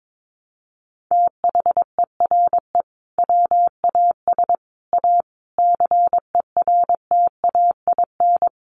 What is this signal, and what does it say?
Morse code, heard as a keyed tone, about 22 wpm, 700 Hz. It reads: THERE WAS A CERTAIN
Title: Sentences from Top 1200 Words 22wpm